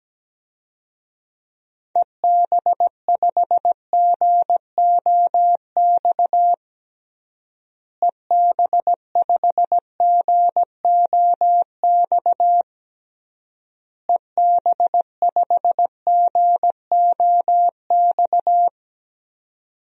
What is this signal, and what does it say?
Morse code, heard as a keyed tone, about 17 wpm, 700 Hz. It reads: EB5GOX EB5GOX EB5GOX
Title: DXCC-100 Call Signs - 6 Characters - Repeated 3x - 17wpm